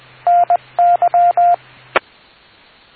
Morse code, heard as a keyed tone, about 20 wpm, 700 Hz, with radio noise. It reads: NY